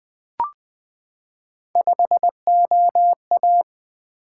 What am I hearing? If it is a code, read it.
Morse code, heard as a keyed tone, about 20 wpm, 700 Hz. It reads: 5OA
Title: Three Letter-Number Combinations - 20wpm